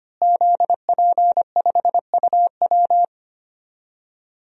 Morse code, heard as a keyed tone, about 25 wpm, 700 Hz. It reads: ZP5UW
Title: DXCC-100 Call Signs - 5 Characters - Repeated 2x - 25wpm